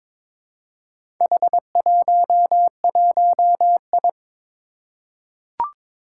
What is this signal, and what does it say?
Morse code, heard as a keyed tone, about 22 wpm, 700 Hz. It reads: H11I
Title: Four Challenging Letter-Number Combinations 22wpm